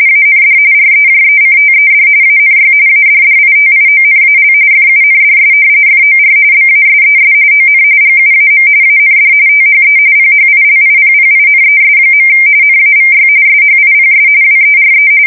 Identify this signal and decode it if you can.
RTTY, about 45.45 baud. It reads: AND REJECTED ANY ADDITIONAL FUNDS FOR BASES ON GUAM AND THE PHILIPPINES. NAVY SHIPS WERE
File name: fsk170 / usnavy4